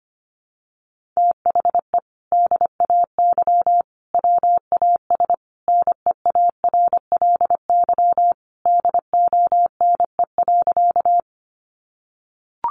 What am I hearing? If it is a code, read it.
Morse code, heard as a keyed tone, about 25 wpm, 700 Hz. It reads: THE DAY WAS NEARLY DONE.
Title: Sentences from Top 1300 Words 25wpm